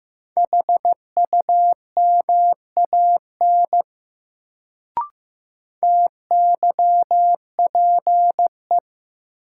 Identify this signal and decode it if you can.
Morse code, heard as a keyed tone, about 15 wpm, 700 Hz. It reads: HUMAN TYPE
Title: Sentences from Top 600 Words - New Word Drill - 15wpm